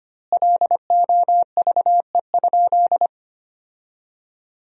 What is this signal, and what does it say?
Morse code, heard as a keyed tone, about 25 wpm, 700 Hz. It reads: LOVE?
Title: 4x Word-Spacing - Sentences from Top 400 Words - 25wpm